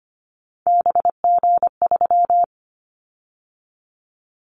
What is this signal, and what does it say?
Morse code, heard as a keyed tone, about 25 wpm, 700 Hz. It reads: BG3
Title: Callsign Prefixes for Top 100 DXCC - 25wpm